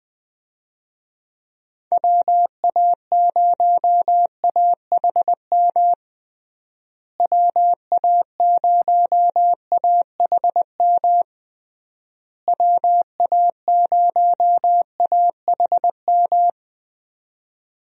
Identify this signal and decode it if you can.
Morse code, heard as a keyed tone, about 21 wpm, 700 Hz. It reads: WA0AHM WA0AHM WA0AHM
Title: USA Call Signs - 6 Characters - Repeated 3x - 20wpm